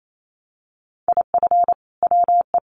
Morse code, heard as a keyed tone, about 28 wpm, 700 Hz. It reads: IF WE
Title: Sentences from Top 1100 Words 28wpm